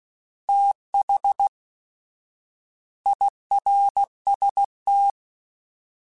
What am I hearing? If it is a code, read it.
Morse code, heard as a keyed tone, about 16 wpm, spaced as for 11 wpm, 800 Hz. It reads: THIRST
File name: TH / TH mp3s / Fast TH @15